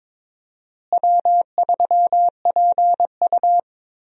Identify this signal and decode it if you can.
Morse code, heard as a keyed tone, about 22 wpm, 700 Hz. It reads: W3PU
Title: USA Call Signs - 4 Characters - Repeated 2x - 22wpm